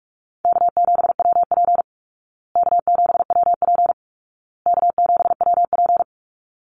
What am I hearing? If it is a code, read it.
Morse code, heard as a keyed tone, about 46 wpm, 700 Hz. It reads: K7WP K7WP K7WP
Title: Easy Call Signs Repeated 3x - 45wpm